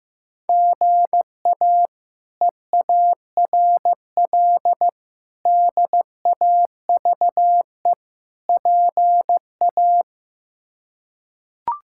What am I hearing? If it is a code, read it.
Morse code, heard as a keyed tone, about 15 wpm, 700 Hz. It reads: GA EARL DAVE PA
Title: K1USN SST QSO Practice - 15wpm